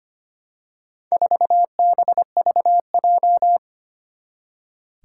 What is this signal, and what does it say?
Morse code, heard as a keyed tone, about 25 wpm, 700 Hz. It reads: 4BVJ
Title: Four Challenging Letter-Number Combinations 25wpm